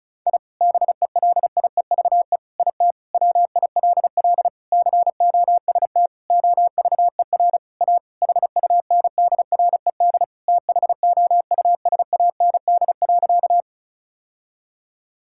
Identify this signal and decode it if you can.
Morse code, heard as a keyed tone, about 36 wpm, 700 Hz. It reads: I BELIEVE IT WILL COST OVER A HUNDRED THOUSAND.
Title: Sentences from Top 1000 Words - Review Part 5 - 35wpm